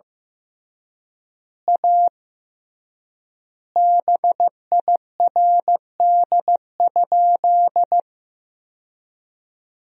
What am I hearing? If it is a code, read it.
Morse code, heard as a keyed tone, about 15 wpm, 700 Hz. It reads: A BIRD?
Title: 3x Word-Spacing - Sentences from Top 400 Words - 15wpm